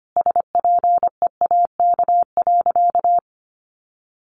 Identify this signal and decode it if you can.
Morse code, heard as a keyed tone, about 25 wpm, 700 Hz. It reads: SPEAK.
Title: Sentences from Top 1100 Words 25wpm